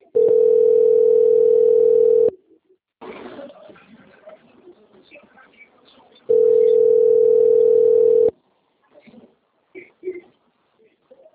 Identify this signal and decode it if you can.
Ringback tone (North America), 440 + 480 Hz.